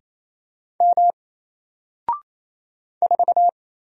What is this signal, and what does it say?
Morse code, heard as a keyed tone, about 27 wpm, 700 Hz. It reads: M 4
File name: Single-Letter-Number-28wpm